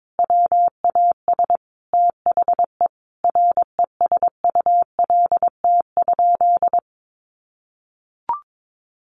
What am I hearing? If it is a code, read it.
Morse code, heard as a keyed tone, about 22 wpm, 700 Hz. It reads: WAS THE RESULT?
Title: Sentences from Top 1500 Words 22wpm